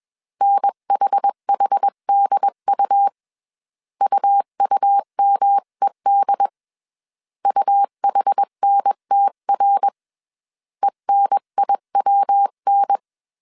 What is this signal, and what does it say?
Morse code, heard as a keyed tone, about 21 wpm, 800 Hz. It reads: NHHDU UUMED UHNTR ENIWN